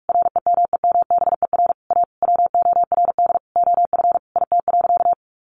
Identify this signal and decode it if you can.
Morse code, heard as a keyed tone, about 44 wpm, 700 Hz. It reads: REMEMBER A WORD OF IT.
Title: Sentences from Top 1200 Words 45wpm